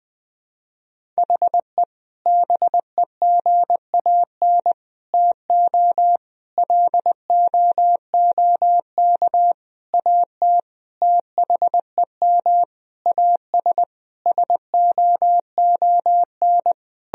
Morse code, heard as a keyed tone, about 20 wpm, 700 Hz. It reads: HE BEGAN TO LOOK AT THEM AS SOON
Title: Sentences from Top 1000 Words - Review Part 1 - 20wpm